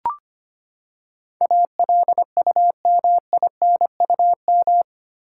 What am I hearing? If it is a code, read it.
Morse code, heard as a keyed tone, about 25 wpm, 700 Hz. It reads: ALUMINUM
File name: Eight Letter Words - 25wpm